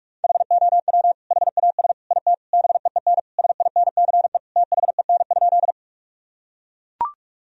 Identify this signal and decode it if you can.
Morse code, heard as a keyed tone, about 42 wpm, 700 Hz. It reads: HOW HAS IT BEEN SINCE THEN?
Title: Sentences from Top 1000 Words - Review Part 1 - 45wpm